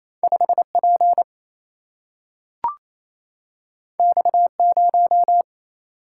Morse code, heard as a keyed tone, about 28 wpm, 700 Hz. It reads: 5 P X 0
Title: Single-Letter Single-Number 28wpm